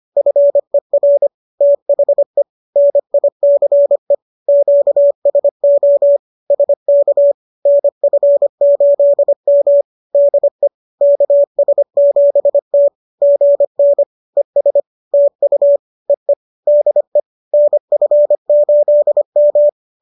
Morse code, heard as a keyed tone, about 25 wpm, 570 Hz. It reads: FER THE NICE QSO SK NF8M DE KS7T GN ES TU EE DE NF8M